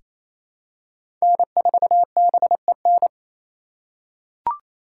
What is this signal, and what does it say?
Morse code, heard as a keyed tone, about 28 wpm, 700 Hz. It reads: N4BEN
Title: USA Call Signs - 5 Characters - 4x Character Spacing - 28wpm